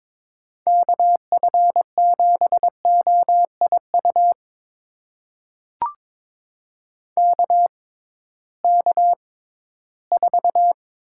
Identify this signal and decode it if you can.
Morse code, heard as a keyed tone, about 22 wpm, spaced as for 14 wpm, 700 Hz. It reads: KF7OIU KK4
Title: USA Call Signs - 6 Characters - 6x Character Spacing - 22wpm